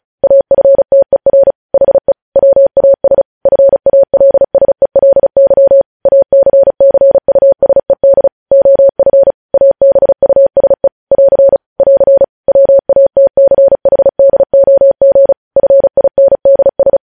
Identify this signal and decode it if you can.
Morse code, heard as a keyed tone, about 36 wpm, 550 Hz. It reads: AFTER HE WAS FALSELY ACCUSED OF ABUSE <AR> <AR> WATCHDOG FINDS
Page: news headlines in morse code at 35 wpm